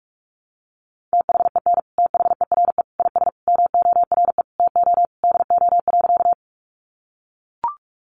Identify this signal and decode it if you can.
Morse code, heard as a keyed tone, about 44 wpm, 700 Hz. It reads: THEN THERE IS MORE TO DO.
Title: Sentences from Top 100 Words 45wpm